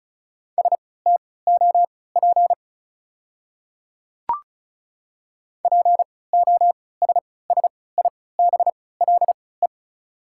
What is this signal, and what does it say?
Morse code, heard as a keyed tone, about 36 wpm, spaced as for 20 wpm, 690 Hz. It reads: STOP POSSIBLE
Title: 3x Character Spacing - Top 500 Words - 35wpm